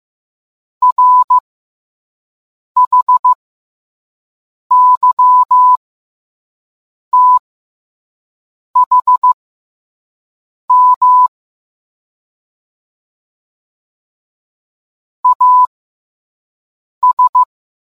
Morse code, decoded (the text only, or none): RHYTHM AS